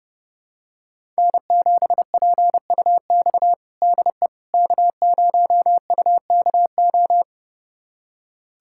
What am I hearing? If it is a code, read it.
Morse code, heard as a keyed tone, about 30 wpm, 700 Hz. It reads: N7PUX DE K0UKO
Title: USA Callsign DE Callsign - 5 Characters - 30wpm